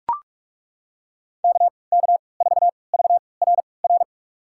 Morse code, heard as a keyed tone, about 43 wpm, spaced as for 25 wpm, 700 Hz. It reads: KK4VRR
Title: USA Call Signs - 3x Character Spacing - 45wpm